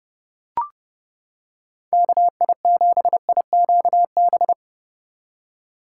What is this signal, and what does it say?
Morse code, heard as a keyed tone, about 30 wpm, 700 Hz. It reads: KI7IQB